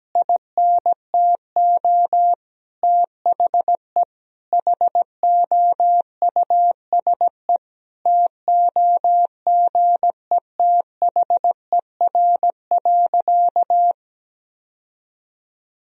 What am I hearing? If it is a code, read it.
Morse code, heard as a keyed tone, about 17 wpm, 700 Hz. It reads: INTO THE HOUSE TOGETHER.
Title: Sentences from Top 1000 Words - Review Part 1 - 17wpm